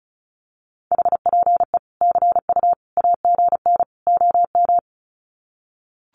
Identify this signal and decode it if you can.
Morse code, heard as a keyed tone, about 34 wpm, 700 Hz. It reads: HPE CU AGN OM
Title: Common QSO Phrases 35wpm